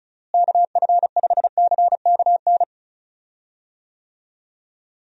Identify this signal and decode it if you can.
Morse code, heard as a keyed tone, about 34 wpm, 700 Hz. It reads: KF5CKN
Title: USA Call Signs - 6 Characters - Repeated 2x - 35wpm